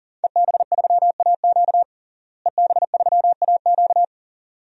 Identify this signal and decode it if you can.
Morse code, heard as a keyed tone, about 37 wpm, 700 Hz. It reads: EB3AQ EB3AQ
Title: DXCC-100 Call Signs - 5 Characters - Repeated 2x - 40wpm